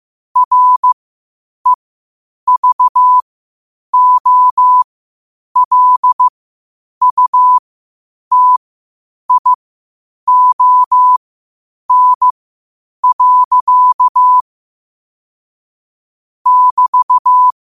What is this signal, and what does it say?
Morse code, heard as a keyed tone, about 15 wpm, spaced as for 8 wpm, 1000 Hz. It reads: REVOLUTION. <BT>
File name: qotdfiles / 8wpm